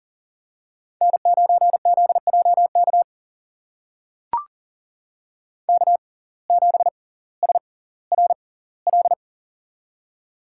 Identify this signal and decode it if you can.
Morse code, heard as a keyed tone, about 40 wpm, spaced as for 20 wpm, 700 Hz. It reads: N9ZJK K7SRL
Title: USA Call Signs - 5 Characters - 6x Character Spacing - 40wpm